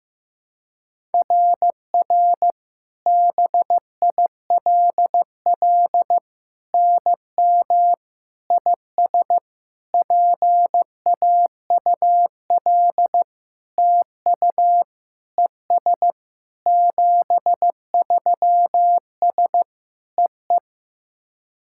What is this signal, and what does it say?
Morse code, heard as a keyed tone, about 15 wpm, 700 Hz. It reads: RR BILL NM IS PAUL TU ES 73S EE
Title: SOTA QSO Practice 15wpm